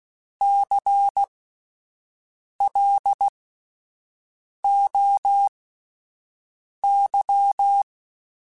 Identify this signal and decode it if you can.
Morse code, heard as a keyed tone, about 16 wpm, spaced as for 5 wpm, 800 Hz. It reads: CLOY
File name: Alphabetical / Alphabetical mp3s / Cs @5